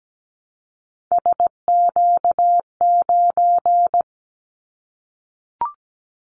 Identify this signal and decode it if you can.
Morse code, heard as a keyed tone, about 17 wpm, 700 Hz. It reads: SQ9
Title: Callsign Prefixes for Top 25 DXCC - Repeated 3x - 17wpm